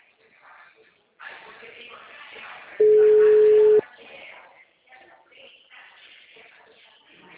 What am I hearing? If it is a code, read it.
Ringback tone (Europe, 425 Hz), with speech.